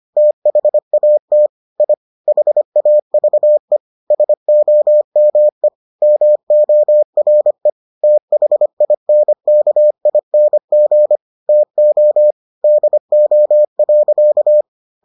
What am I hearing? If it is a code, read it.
Morse code, heard as a keyed tone, about 25 wpm, 600 Hz. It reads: THAT I HAVE SOME MORE THINKING TO DO.